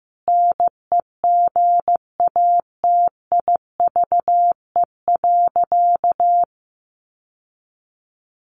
Morse code, heard as a keyed tone, about 15 wpm, 700 Hz. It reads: NEGATIVE.
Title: Sentences from Top 1000 Words 15wpm